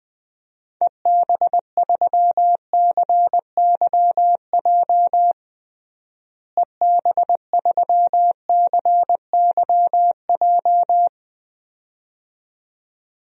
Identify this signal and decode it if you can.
Morse code, heard as a keyed tone, about 20 wpm, 700 Hz. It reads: EB3CYJ EB3CYJ
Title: DXCC-100 Call Signs - 6 Characters - Repeated 2x - 20wpm